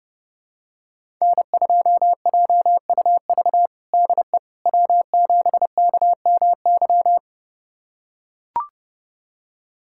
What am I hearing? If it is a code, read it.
Morse code, heard as a keyed tone, about 30 wpm, 700 Hz. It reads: N2JUV DE W7KMY
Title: USA Callsign DE Callsign - 5 Characters - 30wpm